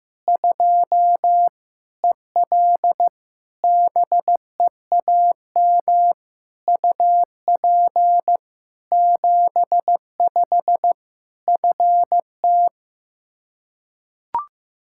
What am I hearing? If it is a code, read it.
Morse code, heard as a keyed tone, about 15 wpm, 700 Hz. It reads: 2 EL BEAM UP 75 FT
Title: Practice QSO - Exchange 2 - Info - 15wpm